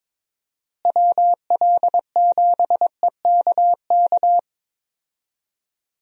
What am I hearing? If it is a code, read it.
Morse code, heard as a keyed tone, about 22 wpm, 700 Hz. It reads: WL7EKK
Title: USA Call Signs - 6 Characters - Repeated 3x - 22wpm